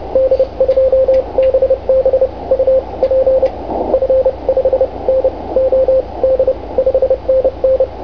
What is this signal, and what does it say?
Morse code, heard as a keyed tone, about 30 wpm, spaced as for 24 wpm, 530 Hz, with radio noise. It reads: D2BB UP F5NOD 5NN